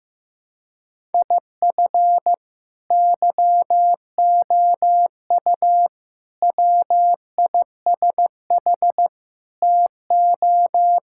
Morse code, heard as a keyed tone, about 15 wpm, 700 Hz. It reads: IF YOU WISH TO
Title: Sentences from Top 1600 Words 15wpm